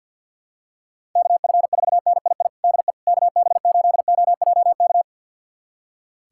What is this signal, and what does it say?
Morse code, heard as a keyed tone, about 50 wpm, 700 Hz. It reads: KF4NII DE KB8OJK
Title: USA Callsign DE Callsign - 6 Characters - 50wpm